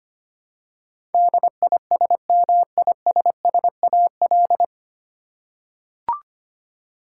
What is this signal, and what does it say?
Morse code, heard as a keyed tone, about 25 wpm, 700 Hz. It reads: DISMISSAL
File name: practice / 25wpm / 9 Letter Words - First 3 Letters Given - 25wpm